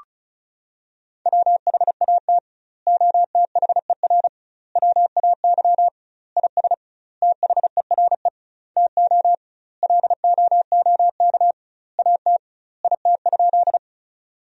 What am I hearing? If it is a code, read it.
Morse code, heard as a keyed tone, about 36 wpm, 700 Hz. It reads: WHAT OTHER WAY IS THERE TO LOOK AT IT?
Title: 2x Word-Spacing - Sentences from Top 100 Words - 35wpm